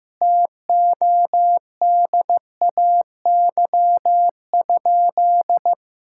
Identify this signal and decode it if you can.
Morse code, heard as a keyed tone, about 15 wpm, 700 Hz. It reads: TODAY?
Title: Sentences from Top 1000 Words - Review Part 5 - 15wpm